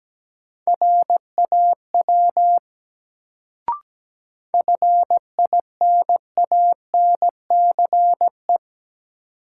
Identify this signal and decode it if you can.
Morse code, heard as a keyed tone, about 17 wpm, 700 Hz. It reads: RAW FINANCE
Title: Sentences from Top 1500 Words - New Word Drill - 17wpm